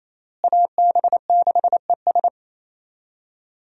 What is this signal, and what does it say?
Morse code, heard as a keyed tone, about 27 wpm, 700 Hz. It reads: AB6ES
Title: USA Call Signs - 5 Characters - Repeated 2x - 28wpm